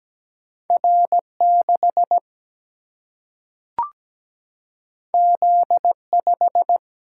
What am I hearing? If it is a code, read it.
Morse code, heard as a keyed tone, about 17 wpm, 700 Hz. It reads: R 6 Z 5
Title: Single-Letter Single-Number 17wpm